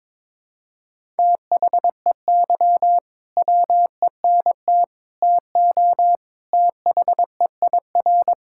Morse code, heard as a keyed tone, about 21 wpm, 700 Hz. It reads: THEY WENT TO THEIR
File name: practice / 22wpm / Sentences with 2-Word Phrases - 22wpm